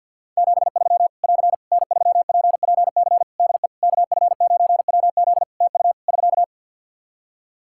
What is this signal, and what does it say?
Morse code, heard as a keyed tone, about 52 wpm, 700 Hz. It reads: 73 <AR> N3PPC DE KF9WZ TU <SK>
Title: K1USN SST QSO Practice 50wpm